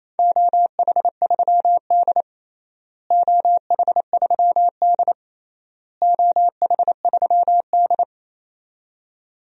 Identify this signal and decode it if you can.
Morse code, heard as a keyed tone, about 28 wpm, 700 Hz. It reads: OH3D OH3D OH3D
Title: DXCC-100 Call Signs - 4 Characters - Repeated 3x - 28wpm